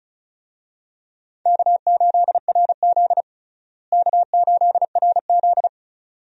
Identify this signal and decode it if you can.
Morse code, heard as a keyed tone, about 36 wpm, 700 Hz. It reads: K8RZ K8RZ
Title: USA Call Signs - 4 Characters - Repeated 2x - 35wpm